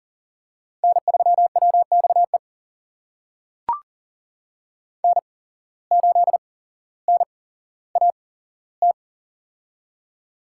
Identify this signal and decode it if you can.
Morse code, heard as a keyed tone, about 38 wpm, spaced as for 17 wpm, 700 Hz. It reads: N3WXE N8NAT
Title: USA Call Signs - 8x Character Spacing - 40wpm